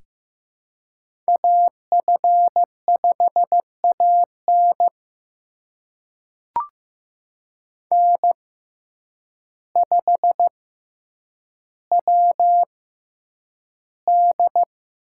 Morse code, heard as a keyed tone, about 15 wpm, spaced as for 8 wpm, 700 Hz. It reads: AF5AN N5WD
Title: USA Call Signs - 5 Characters - 6x Character Spacing - 15wpm